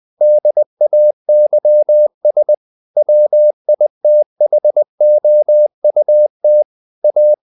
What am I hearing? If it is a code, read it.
Morse code, heard as a keyed tone, about 20 wpm, 600 Hz. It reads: DAYS WITHOUT A
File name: the old man and the sea / the old man and the sea part 1 20wpm